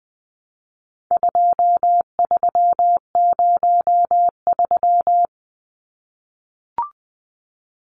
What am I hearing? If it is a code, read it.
Morse code, heard as a keyed tone, about 20 wpm, 700 Hz. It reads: 2303
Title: Numbers - 4-Digit 20wpm